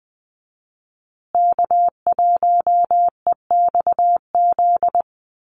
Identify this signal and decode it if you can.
Morse code, heard as a keyed tone, about 20 wpm, 700 Hz. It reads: K1EXZ
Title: USA Call Signs Repeated 2x 20wpm